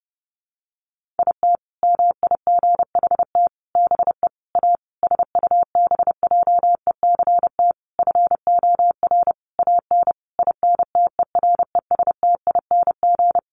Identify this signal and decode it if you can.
Morse code, heard as a keyed tone, about 30 wpm, 700 Hz. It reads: IT MIGHT BE A SUBJECT FOR AN INTERESTING
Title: Sentences from Top 1500 Words 30wpm